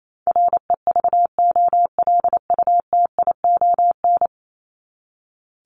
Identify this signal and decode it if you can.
Morse code, heard as a keyed tone, about 28 wpm, 700 Hz. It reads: REVOLUTION
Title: Top 1751-2000 Words 28wpm